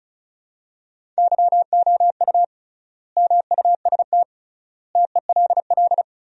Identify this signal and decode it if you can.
Morse code, heard as a keyed tone, about 34 wpm, 700 Hz. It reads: YOU MUST TELL
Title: 3x Word-Spacing - Sentences from Top 200 Words - 35wpm